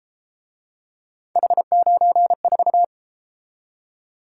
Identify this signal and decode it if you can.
Morse code, heard as a keyed tone, about 33 wpm, 700 Hz. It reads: H94